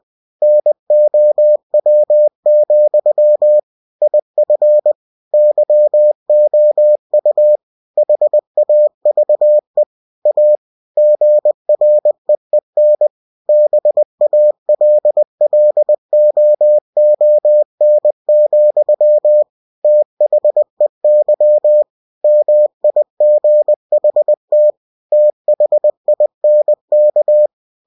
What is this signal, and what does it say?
Morse code, heard as a keyed tone, about 20 wpm, 600 Hz. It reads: NOW, IF YOU HAVE A GREEN BALLOON, THEY MIGHT THINK
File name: cw-books / winnie-the-pooh / winnie-20wpm-page-0005